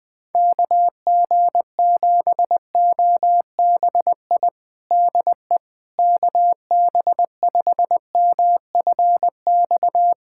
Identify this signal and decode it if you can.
Morse code, heard as a keyed tone, about 20 wpm, 700 Hz. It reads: KG7OBI DE KB5MFX
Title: USA Callsign DE Callsign - 6 Characters - 20wpm